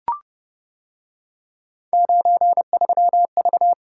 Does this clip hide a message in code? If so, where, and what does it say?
Yes: Morse code from 1.9 to 3.7 s, heard as a keyed tone, about 28 wpm, 700 Hz. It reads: 93V